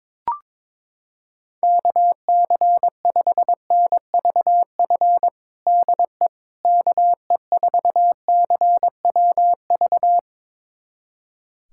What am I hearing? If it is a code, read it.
Morse code, heard as a keyed tone, about 22 wpm, 700 Hz. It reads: KC5NVF DE KE4CWV
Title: USA Callsign DE Callsign 22wpm